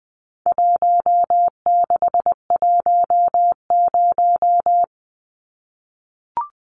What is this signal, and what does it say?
Morse code, heard as a keyed tone, about 20 wpm, 700 Hz. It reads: 1610
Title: Numbers - 4-Digit 20wpm